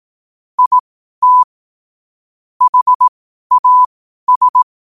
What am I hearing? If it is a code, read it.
Morse code, heard as a keyed tone, about 18 wpm, spaced as for 12 wpm, 1000 Hz. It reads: IT HAS
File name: qotdfiles / 12wpm